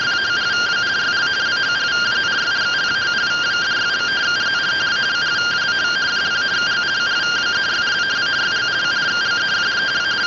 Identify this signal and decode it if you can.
Two-tone teleprinter signal, about 170 Hz shift.